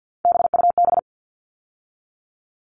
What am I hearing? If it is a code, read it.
Morse code, heard as a keyed tone, about 50 wpm, 700 Hz. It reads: BUB
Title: All Cut Numbers - 3-Digit 50wpm